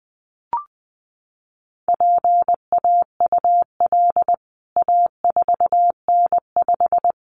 Morse code, heard as a keyed tone, about 20 wpm, 700 Hz. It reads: PAUL A4N5
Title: CWT Exchange 20wpm